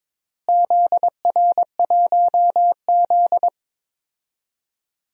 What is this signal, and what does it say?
Morse code, heard as a keyed tone, about 22 wpm, 700 Hz. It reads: ZR1Z
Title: DXCC-100 Call Signs - 4 Characters - Repeated 3x - 22wpm